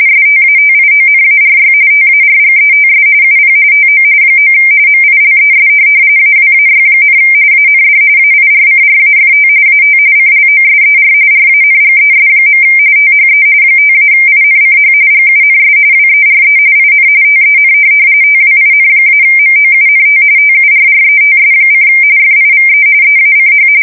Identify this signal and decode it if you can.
RTTY, about 45.45 baud. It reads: STATIONARY TARGETS UP TO 9 MILES AWAY, AND AIRCRAFT AT CLOSER RANGES. THE 5-INCH GUNS WERE IN TEN TWIN-MOUNT TURRETS: FIVE EACH
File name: fsk170 / USS-New-Jersey